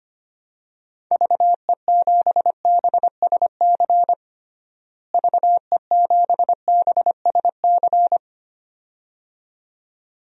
Morse code, heard as a keyed tone, about 25 wpm, 700 Hz. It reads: VE7BSC VE7BSC
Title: DXCC-100 Call Signs - 6 Characters - Repeated 2x - 25wpm